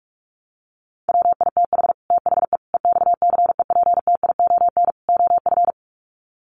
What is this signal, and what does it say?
Morse code, heard as a keyed tone, about 44 wpm, 700 Hz. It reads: WITH THE EXCEPTION OF
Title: Sets of 4 Words 45wpm